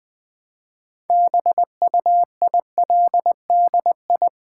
Morse code, heard as a keyed tone, about 20 wpm, 700 Hz. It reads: BUILDI
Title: Build A Word - 2 Letters At A Time - 8-Letter Words - 20wpm